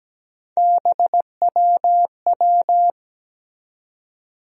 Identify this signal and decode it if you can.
Morse code, heard as a keyed tone, about 17 wpm, 700 Hz. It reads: BWW
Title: USA Call Sign Postfixes - 3 Letters 17wpm